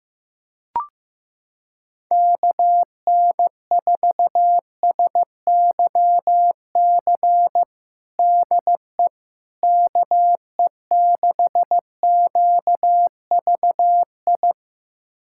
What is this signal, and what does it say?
Morse code, heard as a keyed tone, about 15 wpm, 700 Hz. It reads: KN4SYC DE KE6QVI
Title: USA Callsign DE Callsign - 6 Characters - 15wpm